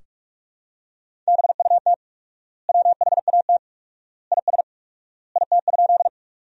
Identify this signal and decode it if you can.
Morse code, heard as a keyed tone, about 41 wpm, 700 Hz. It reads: BUT WHAT IS IT?
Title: 4x Word-Spacing - Sentences from Top 100 Words - 45wpm